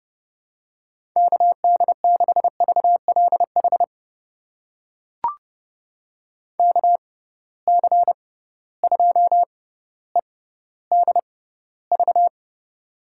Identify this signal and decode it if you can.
Morse code, heard as a keyed tone, about 30 wpm, spaced as for 15 wpm, 700 Hz. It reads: KD6VLH KC2EDV
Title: USA Call Signs - 6 Characters - 6x Character Spacing - 30wpm